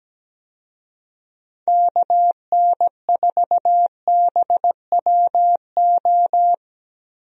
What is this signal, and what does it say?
Morse code, heard as a keyed tone, about 17 wpm, 700 Hz. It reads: KN4BWO
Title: USA Call Signs - 6 Characters - Repeated 2x - 17wpm